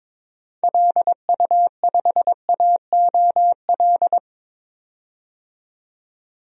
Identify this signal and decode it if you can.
Morse code, heard as a keyed tone, about 22 wpm, 700 Hz. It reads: LU5AOL
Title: DXCC-100 Call Signs - 6 Characters - 22wpm